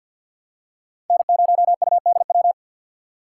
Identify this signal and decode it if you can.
Morse code, heard as a keyed tone, about 48 wpm, 700 Hz. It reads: N0UDW
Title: USA Call Signs - 5 Characters - 3x Character Spacing - 50wpm